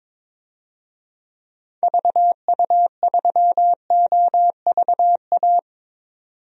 Morse code, heard as a keyed tone, about 22 wpm, 700 Hz. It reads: VU3OVA
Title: DXCC-100 Call Signs - 6 Characters - 22wpm